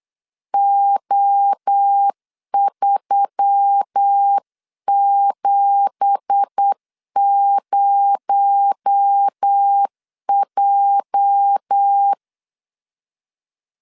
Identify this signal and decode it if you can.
Morse code, heard as a keyed tone, about 8 wpm, 800 Hz. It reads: O370J
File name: LEZIONI / 10 SIGNS / 10B INT 8wpm 3534